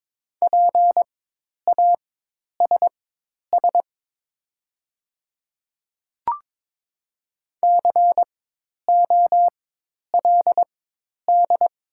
Morse code, heard as a keyed tone, about 22 wpm, spaced as for 10 wpm, 700 Hz. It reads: PASS COLD
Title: 4x Character Spacing - Top 500 Words - 22wpm